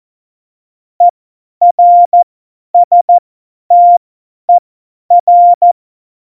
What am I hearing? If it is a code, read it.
Morse code, heard as a keyed tone, about 14 wpm, spaced as for 10 wpm, 700 Hz. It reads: ERSTER
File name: Morsekurs / Witze / Witz 21 Juli 2020, 50 F70